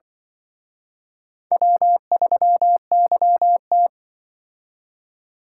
Morse code, heard as a keyed tone, about 25 wpm, 700 Hz. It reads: W3YT
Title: Speed Racing - USA Call Signs 20wpm